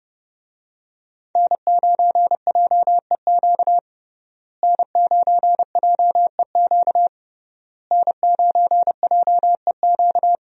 Morse code, heard as a keyed tone, about 30 wpm, 700 Hz. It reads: N9JEQ N9JEQ N9JEQ
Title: USA Call Signs - 5 Characters - Repeated 3x - 30wpm